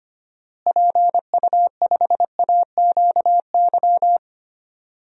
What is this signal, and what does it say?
Morse code, heard as a keyed tone, about 25 wpm, 700 Hz. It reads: PU5AQY
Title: DXCC-100 Call Signs - 6 Characters - Repeated 2x - 25wpm